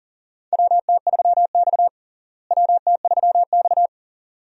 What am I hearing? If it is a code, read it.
Morse code, heard as a keyed tone, about 40 wpm, 700 Hz. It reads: WT3X WT3X
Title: USA Call Signs - 4 Characters - Repeated 2x - 40wpm